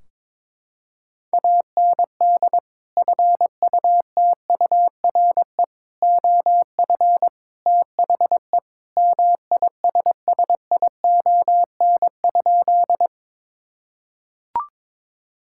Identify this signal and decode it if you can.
Morse code, heard as a keyed tone, about 22 wpm, 700 Hz. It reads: AND FUTURE OF THE MISSION?
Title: Sentences from Top 1600 Words 22wpm